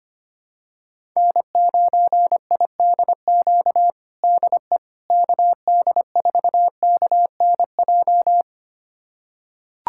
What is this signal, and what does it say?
Morse code, heard as a keyed tone, about 25 wpm, 700 Hz. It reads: N9IDQ DE KD4KNJ
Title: USA Callsign DE Callsign 25wpm